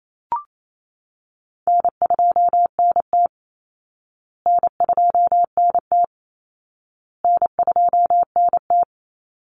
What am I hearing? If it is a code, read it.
Morse code, heard as a keyed tone, about 28 wpm, 700 Hz. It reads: N2NT N2NT N2NT
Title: Call Signs Repeated 2x - 28wpm